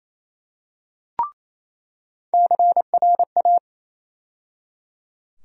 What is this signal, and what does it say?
Morse code, heard as a keyed tone, about 28 wpm, 700 Hz. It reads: CRA
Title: USA Call Sign Postfixes - 3 Letters 28wpm